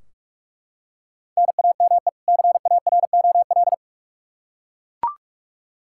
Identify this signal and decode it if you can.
Morse code, heard as a keyed tone, about 41 wpm, 700 Hz. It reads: NAME CAROL
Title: K1USN SST QSO Practice 45wpm